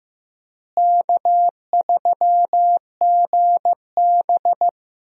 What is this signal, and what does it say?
Morse code, heard as a keyed tone, about 15 wpm, 700 Hz. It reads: K3GB
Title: USA Call Signs - 4 Characters - Repeated 3x - 15wpm